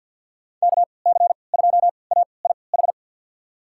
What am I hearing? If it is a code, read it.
Morse code, heard as a keyed tone, about 51 wpm, spaced as for 28 wpm, 700 Hz. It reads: KC2AIH
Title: USA Call Signs - 6 Characters - 3x Character Spacing - 50wpm